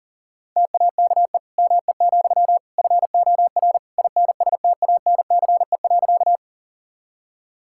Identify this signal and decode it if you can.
Morse code, heard as a keyed tone, about 37 wpm, 700 Hz. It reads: TAKE ME, FOR INSTANCE.
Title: Sentences from Top 1000 Words - Review Part 5 - 40wpm